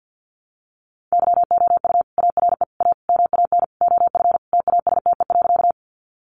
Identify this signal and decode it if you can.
Morse code, heard as a keyed tone, about 50 wpm, 700 Hz. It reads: YOU ARE A MAN OF TASTE.
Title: Sentences from Top 1100 Words 50wpm